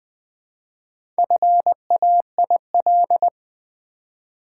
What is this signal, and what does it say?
Morse code, heard as a keyed tone, about 21 wpm, 700 Hz. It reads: FAIL
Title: Sentences from Top 1100 Words - New Word Drill v2 - 20wpm